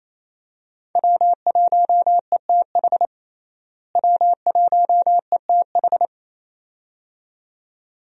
Morse code, heard as a keyed tone, about 28 wpm, 700 Hz. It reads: W1ETH W1ETH
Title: USA Call Signs - 5 Characters - Repeated 2x - 28wpm